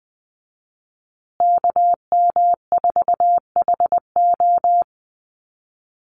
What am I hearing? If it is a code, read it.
Morse code, heard as a keyed tone, about 19 wpm, 700 Hz. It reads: KM4HO
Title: USA Call Signs Repeated 3x 20wpm